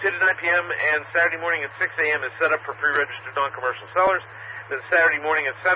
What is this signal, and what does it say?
Emergency Alert System (SAME) tones (header bursts), heard as harsh buzzing bursts, with speech. It is heard from 0.9 to 5.2 s.